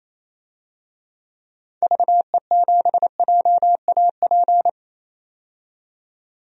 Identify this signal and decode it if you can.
Morse code, heard as a keyed tone, about 28 wpm, 700 Hz. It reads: VE7JAP
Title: DXCC-100 Call Signs - 6 Characters - 28wpm